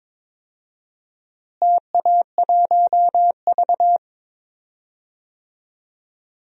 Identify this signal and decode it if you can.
Morse code, heard as a keyed tone, about 22 wpm, 700 Hz. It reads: TA1V
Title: DXCC-100 Call Signs - 4 Characters - Repeated 3x - 22wpm